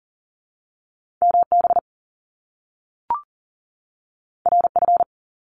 Morse code, heard as a keyed tone, about 38 wpm, 700 Hz. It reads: MB RF
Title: Top Two Letter Combinations in Words 40wpm